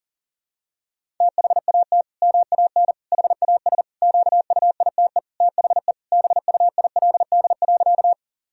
Morse code, heard as a keyed tone, about 40 wpm, 700 Hz. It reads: THAT MAN HAS QUITE THE BUILD.
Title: Sentences from Top 1000 Words - Review Part 1 - 40wpm